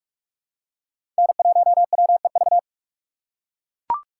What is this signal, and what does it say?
Morse code, heard as a keyed tone, about 43 wpm, 700 Hz. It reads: N1WEV
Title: USA Call Signs - 8x Character Spacing - 45wpm